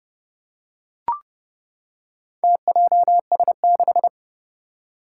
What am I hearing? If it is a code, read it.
Morse code, heard as a keyed tone, about 30 wpm, 700 Hz. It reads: TJS6
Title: Four Challenging Letter-Number Combinations 30wpm